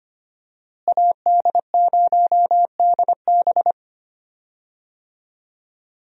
Morse code, heard as a keyed tone, about 25 wpm, 700 Hz. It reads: AD0DB
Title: USA Call Signs - 5 Characters - Repeated 2x - 25wpm